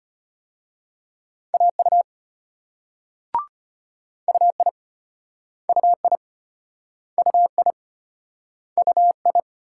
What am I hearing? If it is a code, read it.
Morse code, heard as a keyed tone, about 38 wpm, 700 Hz. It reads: AU UI UI UI UI